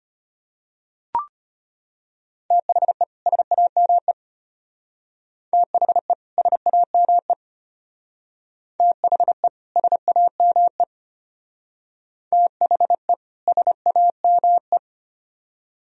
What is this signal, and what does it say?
Morse code, heard as a keyed tone, about 31 wpm, 700 Hz. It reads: THE SAME THE SAME THE SAME THE SAME
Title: Speed-Racing - Sets of 2 Words 25wpm